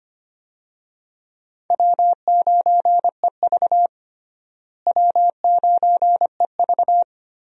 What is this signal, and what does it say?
Morse code, heard as a keyed tone, about 25 wpm, 700 Hz. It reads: W9EV W9EV
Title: USA Call Signs - 4 Characters - Repeated 2x - 25wpm